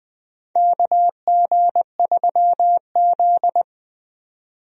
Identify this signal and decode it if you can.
Morse code, heard as a keyed tone, about 20 wpm, 700 Hz. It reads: KG3Z
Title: USA Call Signs - 4 Characters - 20wpm